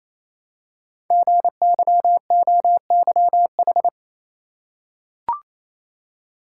Morse code, heard as a keyed tone, about 28 wpm, 700 Hz. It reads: GYOYH
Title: Code Groups - 5 Letter:Numbers - 28wpm